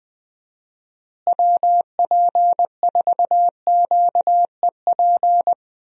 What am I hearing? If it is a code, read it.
Morse code, heard as a keyed tone, about 20 wpm, 700 Hz. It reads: WP4QEP
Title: USA Call Signs - 6 Characters - Repeated 3x - 20wpm